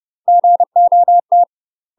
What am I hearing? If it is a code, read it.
Morse code, heard as a keyed tone, about 30 wpm, 700 Hz. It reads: GOT
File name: UK MASTER QSOs TRAINING / 30WPM / UK MASTER QSOs 0048